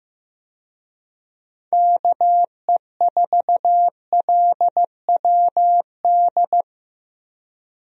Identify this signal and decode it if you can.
Morse code, heard as a keyed tone, about 15 wpm, 700 Hz. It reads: KE4LWD
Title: USA Call Signs - 6 Characters - 15wpm